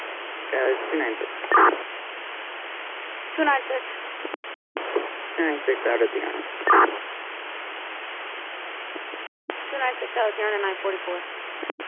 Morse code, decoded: none

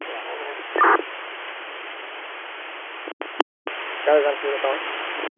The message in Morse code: none

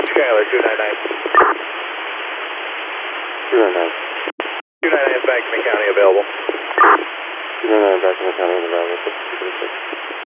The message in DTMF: none

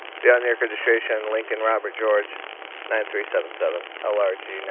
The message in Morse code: none

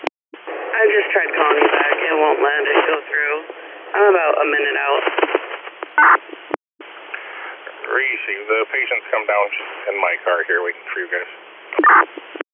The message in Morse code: none

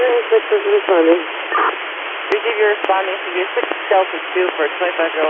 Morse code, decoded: none

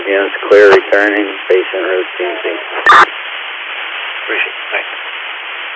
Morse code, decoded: none

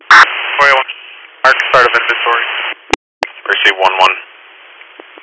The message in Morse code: none

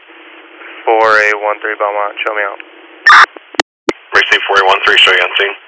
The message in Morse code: none

